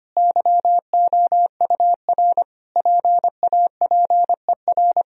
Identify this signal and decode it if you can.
Morse code, heard as a keyed tone, about 25 wpm, 700 Hz. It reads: YOUR PAPER
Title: Sentences from Top 1000 Words - Review Part 2 - 25wpm